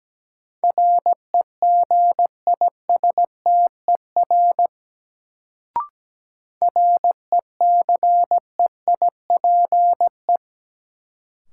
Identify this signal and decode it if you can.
Morse code, heard as a keyed tone, about 17 wpm, 700 Hz. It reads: REGISTER RECEIPE
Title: Sentences from Top 1200 Words - New Word Drill - 17wpm